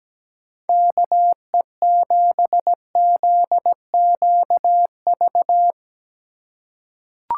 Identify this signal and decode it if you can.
Morse code, heard as a keyed tone, about 17 wpm, 700 Hz. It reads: KE7ZQV
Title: USA Call Signs - 6 Characters - 3x Character Spacing - 17wpm